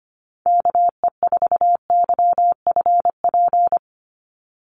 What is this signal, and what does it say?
Morse code, heard as a keyed tone, about 25 wpm, 700 Hz. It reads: KE4YFP
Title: USA Call Signs 25wpm